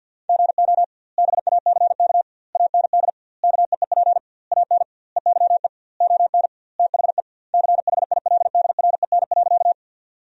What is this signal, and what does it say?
Morse code, heard as a keyed tone, about 50 wpm, 700 Hz. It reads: GO BACK AND KEEP AN EYE ON THE CHILDREN.
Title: 2x Word-Spacing - Sentences from Top 300 Words - 50wpm